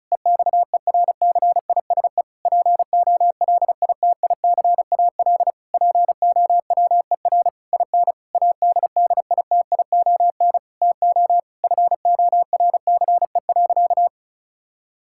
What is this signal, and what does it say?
Morse code, heard as a keyed tone, about 36 wpm, 700 Hz. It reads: EXERCISE POLITICAL POWER IN ADDITION TO FORCE.
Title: Sentences from Top 1000 Words - Review Part 6 - 35wpm